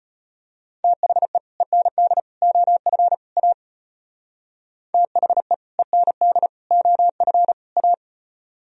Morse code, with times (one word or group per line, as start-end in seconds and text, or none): 0.8-1.4 THE
1.6-2.2 END
2.4-3.1 OF
3.4-3.5 A
4.9-5.5 THE
5.8-6.5 END
6.7-7.5 OF
7.8-7.9 A